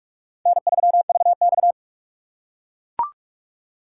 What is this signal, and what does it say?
Morse code, heard as a keyed tone, about 44 wpm, 700 Hz. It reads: N3VX